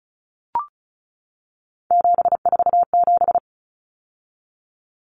Morse code, heard as a keyed tone, about 34 wpm, 700 Hz. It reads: 747
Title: Common Cut Numbers - 3-Digit 35wpm